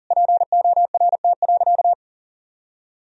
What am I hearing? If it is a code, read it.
Morse code, heard as a keyed tone, about 40 wpm, 700 Hz. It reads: PORT.
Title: Sentences from Top 1000 Words - Review Part 1 - 40wpm